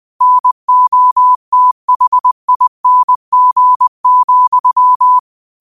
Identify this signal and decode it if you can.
Morse code, heard as a keyed tone, about 20 wpm, 1000 Hz. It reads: NOTHING,
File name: 20wpm